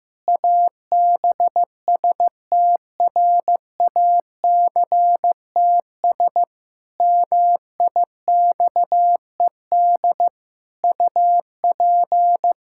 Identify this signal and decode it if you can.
Morse code, heard as a keyed tone, about 15 wpm, 700 Hz. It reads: ABSTRACTS MIXED UP
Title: Pangram Challenge v2 - 15wpm